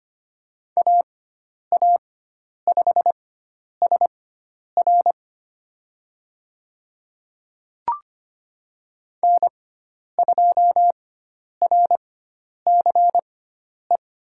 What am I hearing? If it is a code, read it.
Morse code, heard as a keyed tone, about 25 wpm, spaced as for 10 wpm, 700 Hz. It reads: AA5SR N2RCE
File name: Lesson 16g - Letter c - Call Signs 25wpm